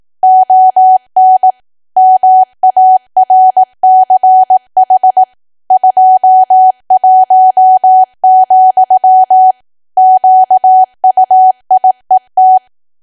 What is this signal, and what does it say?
Morse code, heard as a keyed tone, about 18 wpm, 750 Hz. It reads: ON MARCH 21, QUIET
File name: cwa / W1AW-Level2 / 18WPM